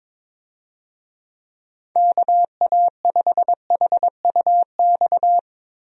Morse code, heard as a keyed tone, about 22 wpm, 700 Hz. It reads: KA5HUX
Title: USA Call Signs - 6 Characters - Repeated 2x - 22wpm